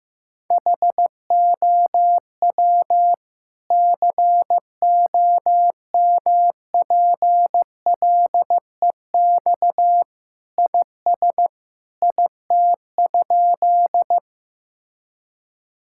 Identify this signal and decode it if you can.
Morse code, heard as a keyed tone, about 15 wpm, 700 Hz. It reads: HOW COMPLEX IS IT?
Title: Sentences from Top 1000 Words - Review Part 6 - 15wpm